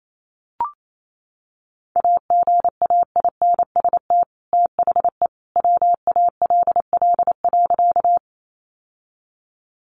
Morse code, heard as a keyed tone, about 28 wpm, 700 Hz. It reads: AGAINST THE WALL.
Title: Sentences from Top 1100 Words 28wpm